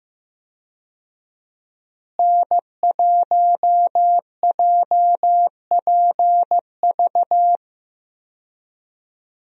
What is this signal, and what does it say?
Morse code, heard as a keyed tone, about 15 wpm, 700 Hz. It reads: N1JPV
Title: USA Call Signs - 5 Characters - Repeated 3x - 15wpm